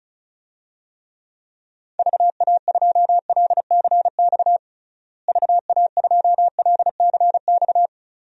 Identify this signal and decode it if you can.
Morse code, heard as a keyed tone, about 34 wpm, 700 Hz. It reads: VA2LCX VA2LCX
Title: DXCC-100 Call Signs - 6 Characters - Repeated 2x - 35wpm